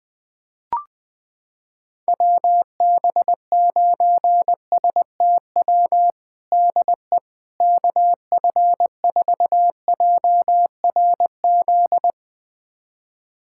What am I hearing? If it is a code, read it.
Morse code, heard as a keyed tone, about 20 wpm, 700 Hz. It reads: WB9STW DE KF4JRZ
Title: USA Callsign DE Callsign 20wpm